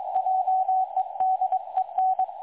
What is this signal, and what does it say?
Morse code, heard as a keyed tone, about 22 wpm, 750 Hz, with radio noise. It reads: JEDEN
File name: dlazaawansowanych / nagraniazpasma / websdr recording 3565.0kHz